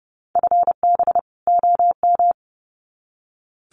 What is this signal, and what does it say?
Morse code, heard as a keyed tone, about 30 wpm, 700 Hz. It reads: FB OM
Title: Common QSO Phrases 30wpm